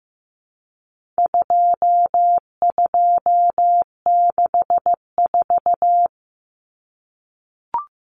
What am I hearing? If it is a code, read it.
Morse code, heard as a keyed tone, about 15 wpm, 700 Hz. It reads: 2264
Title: Numbers - 4-Digit 15wpm